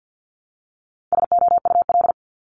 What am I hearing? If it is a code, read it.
Morse code, heard as a keyed tone, about 50 wpm, 700 Hz. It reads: SOUL
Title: Multi-letter Phonemes Repeated 3x 50wpm